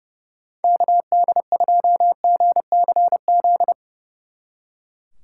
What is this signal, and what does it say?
Morse code, heard as a keyed tone, about 29 wpm, 700 Hz. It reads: KD2GCZ
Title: Speed Racing - USA 2x3 Call Signs 30wpm